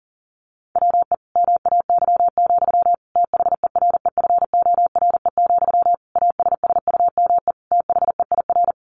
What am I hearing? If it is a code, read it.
Morse code, heard as a keyed tone, about 40 wpm, 700 Hz. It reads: WE MAY, THEREFORE, ASSUME THEIR
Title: Sentences from Top 1500 Words 40wpm